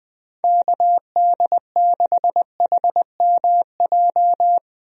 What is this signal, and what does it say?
Morse code, heard as a keyed tone, about 20 wpm, 700 Hz. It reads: KD6HMJ
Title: USA Call Signs - 6 Characters - 20wpm